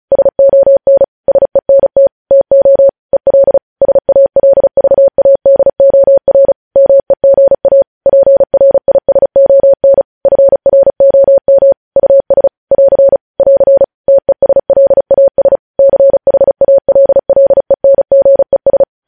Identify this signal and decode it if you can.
Morse code, heard as a keyed tone, about 36 wpm, 550 Hz. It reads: SON SENT TO EL SALVADOR MEGA PRISON FROM US <AR> <AR> TESLAS CHALLENGES
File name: podcast / News headlines-1x-35x35